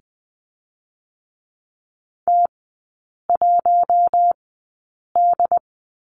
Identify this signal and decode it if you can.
Morse code, heard as a keyed tone, about 20 wpm, spaced as for 8 wpm, 700 Hz. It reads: T1D
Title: 067 Lesson 12e - Letter d - Three Characters 20wpm